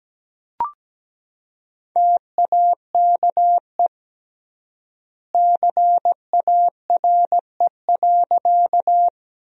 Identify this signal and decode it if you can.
Morse code, heard as a keyed tone, about 17 wpm, 700 Hz. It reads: TAKE CARE.
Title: 3x Word-Spacing - Sentences from Top 300 Words - 17wpm